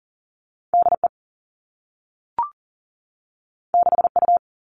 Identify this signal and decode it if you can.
Morse code, heard as a keyed tone, about 40 wpm, 700 Hz. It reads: DE 6U
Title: All Cut Numbers - 2-Digit 40wpm